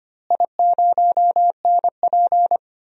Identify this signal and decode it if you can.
Morse code, heard as a keyed tone, about 25 wpm, 700 Hz. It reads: I0NP
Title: DXCC-100 Call Signs - 4 Characters - Repeated 3x - 25wpm